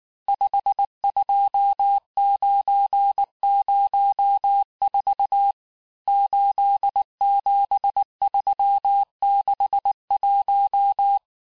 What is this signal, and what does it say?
Morse code, heard as a keyed tone, about 19 wpm, 800 Hz. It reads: 52904 87361
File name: cours-du-mercredi / audios / cours-869-texte-CHIFFRE-vitesse-18-MPM